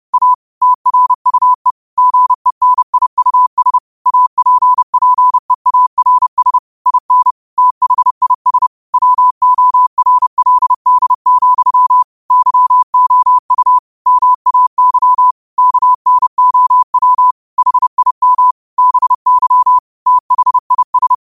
Morse code, heard as a keyed tone, about 30 wpm, 1000 Hz. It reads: A TRUE GENIUS APPEARS IN THIS WORLD, YOU MAY KNOW HIM BY THIS
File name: qotdfiles / 30wpm